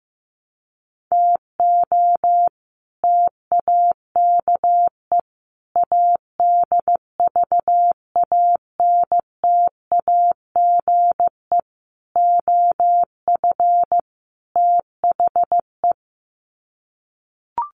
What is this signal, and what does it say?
Morse code, heard as a keyed tone, about 15 wpm, 700 Hz. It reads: TO TAKE ADVANTAGE OF THE
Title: Sets of 5 Words 15wpm